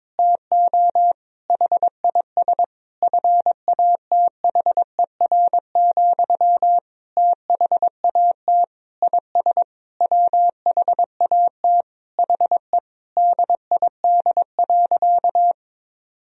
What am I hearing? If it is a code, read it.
Morse code, heard as a keyed tone, about 22 wpm, 700 Hz. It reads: TO HIS FATHER, THAT IS WHAT HE DID.
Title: Sentences from Top 1000 Words - Review Part 4 - 22wpm